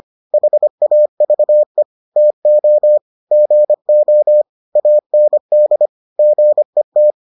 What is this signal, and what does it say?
Morse code, heard as a keyed tone, about 25 wpm, 600 Hz. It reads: HAVE TO GO AND GET